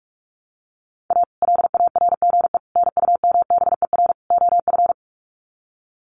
Morse code, heard as a keyed tone, about 44 wpm, 700 Hz. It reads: A LARGE NUMBER OF
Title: Sets of 4 Words 45wpm